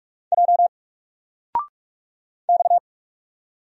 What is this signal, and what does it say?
Morse code, heard as a keyed tone, about 44 wpm, 700 Hz. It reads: J X
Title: Single Letter-Number 45wpm